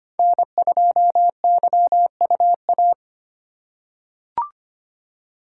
Morse code, heard as a keyed tone, about 25 wpm, 700 Hz. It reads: N2YUA
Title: USA Call Signs - 2x Character Spacing - 25wpm